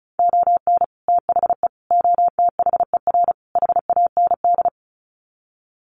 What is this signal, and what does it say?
Morse code, heard as a keyed tone, about 34 wpm, 700 Hz. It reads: ON THE OTHER HAND
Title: Sets of 4 Words 35wpm